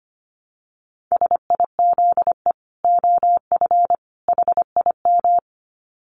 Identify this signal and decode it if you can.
Morse code, heard as a keyed tone, about 25 wpm, 700 Hz. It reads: SIZE OF HIM
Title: Dit Challenge 25wpm